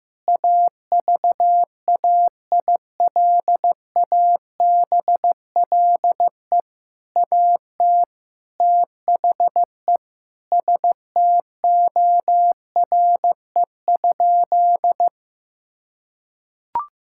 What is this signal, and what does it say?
Morse code, heard as a keyed tone, about 14 wpm, 700 Hz. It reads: AVAILABLE AT THE STORE?
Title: Sentences from Top 1000 Words - Review Part 6 - 15wpm